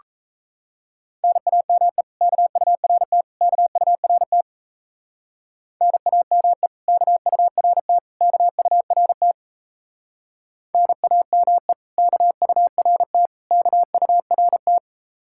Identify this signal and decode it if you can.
Morse code, heard as a keyed tone, about 34 wpm, 700 Hz. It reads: NAME KURT KURT NAME KURT KURT NAME KURT KURT
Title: Speed-Racing Common QSO Phrases - 28wpm